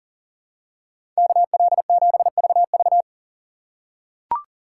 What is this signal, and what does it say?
Morse code, heard as a keyed tone, about 40 wpm, 700 Hz. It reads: KL7VV
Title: USA Call Signs - 5 Characters - 2x Character Spacing - 40wpm